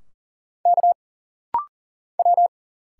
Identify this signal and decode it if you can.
Morse code, heard as a keyed tone, about 40 wpm, 700 Hz. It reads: K W
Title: Single Letters - Rapid-Fire 40wpm